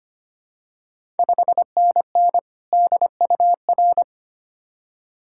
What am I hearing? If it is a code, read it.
Morse code, heard as a keyed tone, about 25 wpm, 690 Hz. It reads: 5NN DUR
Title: Ontario QSO Party Exchange 25wpm